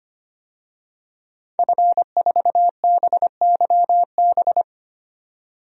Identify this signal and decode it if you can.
Morse code, heard as a keyed tone, about 25 wpm, 700 Hz. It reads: F4BYB
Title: DXCC-100 Call Signs - 5 Characters - 25wpm